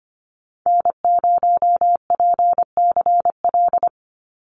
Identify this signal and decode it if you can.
Morse code, heard as a keyed tone, about 25 wpm, 700 Hz. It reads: N0PCL
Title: NA SOTA Activator Callsigns 25wpm